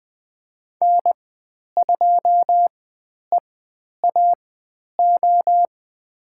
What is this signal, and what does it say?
Morse code, heard as a keyed tone, about 20 wpm, spaced as for 10 wpm, 700 Hz. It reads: N2EAO
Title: 102 Lesson 17g - Letter u - Call Signs 20wpm